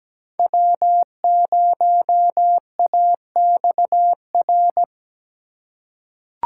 Morse code, heard as a keyed tone, about 17 wpm, 700 Hz. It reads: W0AXR
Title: USA Call Signs - 3x Character Spacing - 17wpm